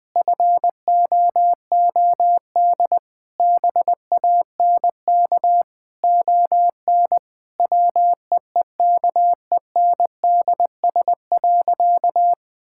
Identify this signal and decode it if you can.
Morse code, heard as a keyed tone, about 21 wpm, 700 Hz. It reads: FOOD BANK ON WEEKENDS.
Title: Difficult Sentences with Hints - 20wpm